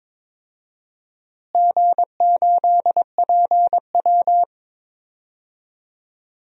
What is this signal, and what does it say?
Morse code, heard as a keyed tone, about 22 wpm, 700 Hz. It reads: G8PW
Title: DXCC-100 Call Signs - 4 Characters - Repeated 3x - 22wpm